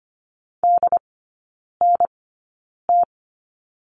Morse code, heard as a keyed tone, about 25 wpm, spaced as for 9 wpm, 690 Hz. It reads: DNT
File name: Lesson 15e - Number 5 - Three Characters 25wpm